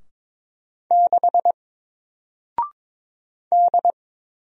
Morse code, heard as a keyed tone, about 22 wpm, 700 Hz. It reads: 6 D
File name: Single-Letter-Number-22wpm